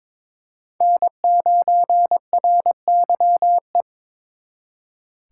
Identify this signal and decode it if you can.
Morse code, heard as a keyed tone, about 26 wpm, 700 Hz. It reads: N9RYE
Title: Speed Racing - USA 1x3 Call Signs 22wpm